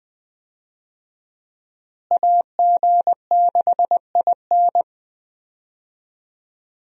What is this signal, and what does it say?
Morse code, heard as a keyed tone, about 20 wpm, 700 Hz. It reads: AG6IN
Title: USA Call Signs - 5 Characters - Repeated 3x - 20wpm